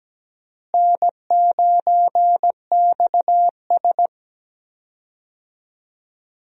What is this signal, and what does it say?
Morse code, heard as a keyed tone, about 17 wpm, 700 Hz. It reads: N9XS
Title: USA Call Signs - 4 Characters - Repeated 3x - 17wpm